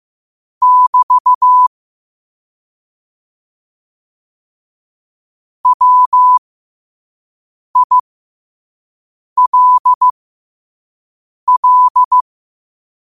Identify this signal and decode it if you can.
Morse code, heard as a keyed tone, about 15 wpm, spaced as for 5 wpm, 1000 Hz. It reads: <BT> WILL
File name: qotdfiles / 5wpm